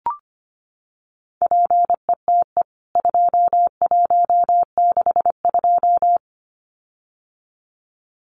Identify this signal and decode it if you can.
Morse code, heard as a keyed tone, about 25 wpm, 700 Hz. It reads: PETE 2162
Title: CWT Exchange 25wpm